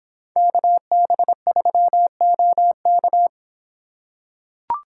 Morse code, heard as a keyed tone, about 26 wpm, 700 Hz. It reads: KB3OK